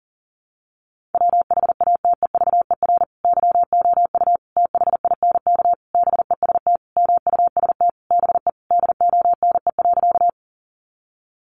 Morse code, heard as a keyed tone, about 40 wpm, 700 Hz. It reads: WHATEVER YOU THINK BEST MUST BE DONE.
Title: Sentences from Top 1100 Words 40wpm